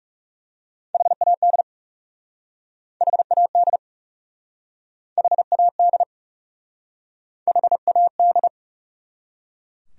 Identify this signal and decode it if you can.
Morse code, heard as a keyed tone, about 40 wpm, 700 Hz. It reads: HAD HAD HAD HAD